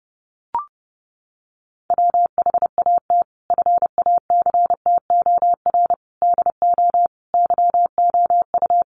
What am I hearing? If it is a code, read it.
Morse code, heard as a keyed tone, about 30 wpm, 700 Hz. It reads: WHAT FACTOR DO YOU
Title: Sentences from Top 1300 Words 30wpm